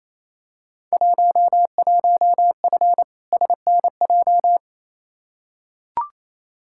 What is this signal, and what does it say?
Morse code, heard as a keyed tone, about 28 wpm, 700 Hz. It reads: 11F SNJ
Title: Field Day Exchanges Encore - 28wpm